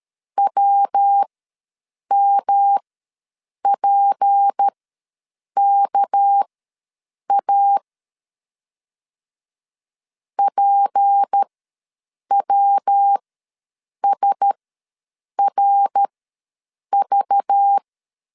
Morse code, heard as a keyed tone, about 13 wpm, spaced as for 7 wpm, 800 Hz. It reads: WMPKA PWSRV
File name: LEZIONI / 6 PKX0 / 6 PK ws 12wpm 3116